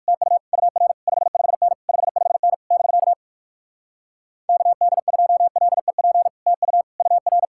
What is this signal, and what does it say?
Morse code, heard as a keyed tone, about 59 wpm, 700 Hz. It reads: TU UR 55N 55N <BK> KD2LEP TU UR